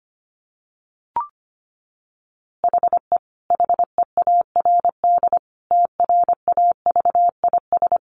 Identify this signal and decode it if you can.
Morse code, heard as a keyed tone, about 25 wpm, 700 Hz. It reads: HE HEARD TRAVIS
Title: A Sound Of Thunder 25wpm